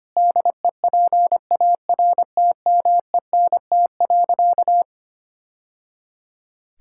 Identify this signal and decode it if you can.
Morse code, heard as a keyed tone, about 25 wpm, 700 Hz. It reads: DEPARTMENT.
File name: practice / 25wpm / Sentences with 3-Word Phrases - 25wpm